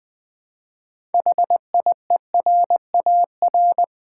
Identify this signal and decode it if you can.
Morse code, heard as a keyed tone, about 20 wpm, 700 Hz. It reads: HIERAR
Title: Build A Word - 3 Letters At A Time - 9-Letter Words - 20wpm